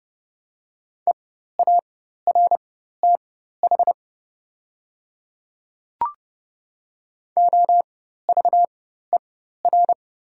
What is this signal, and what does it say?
Morse code, heard as a keyed tone, about 30 wpm, spaced as for 14 wpm, 700 Hz. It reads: EARTH OVER
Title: 4x Character Spacing - Top 500 Words - 30wpm